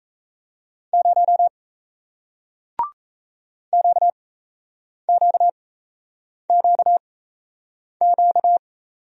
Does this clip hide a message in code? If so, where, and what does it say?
Yes: Morse code from 0.9 to 8.6 s, heard as a keyed tone, about 37 wpm, spaced as for 8 wpm, 700 Hz. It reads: 0 QQQQ